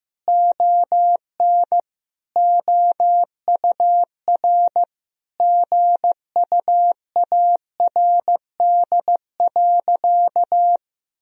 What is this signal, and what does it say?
Morse code, heard as a keyed tone, about 15 wpm, 700 Hz. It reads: ON OUR GUARD.
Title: Sentences from Top 1000 Words - Review Part 4 - 15wpm